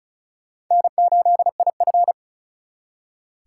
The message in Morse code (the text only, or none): N8IF